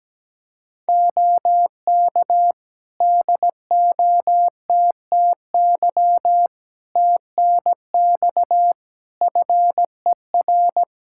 Morse code, heard as a keyed tone, about 17 wpm, 700 Hz. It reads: OK DOTTY TNX FER
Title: K1USN SST QSO Practice 17wpm